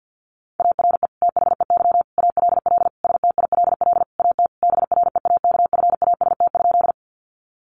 Morse code, heard as a keyed tone, about 50 wpm, 700 Hz. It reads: ARE THEY ALL STILL AT BREAKFAST?
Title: Sentences from Top 1500 Words 50wpm